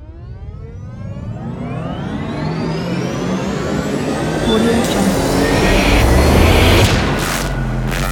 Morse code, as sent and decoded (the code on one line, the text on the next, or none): none